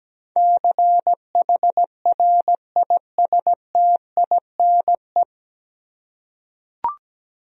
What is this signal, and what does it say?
Morse code, heard as a keyed tone, about 17 wpm, 700 Hz. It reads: CHRISTINE
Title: USA Common Names - 17wpm